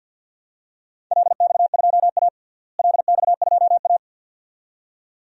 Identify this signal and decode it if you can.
Morse code, heard as a keyed tone, about 51 wpm, 700 Hz. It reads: LX2A LX2A
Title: DXCC-100 Call Signs - 4 Characters - Repeated 2x - 50wpm